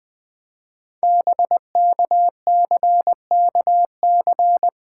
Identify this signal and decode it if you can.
Morse code, heard as a keyed tone, about 20 wpm, 700 Hz. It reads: BKCKC
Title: Code Groups - 5 Letters - 20wpm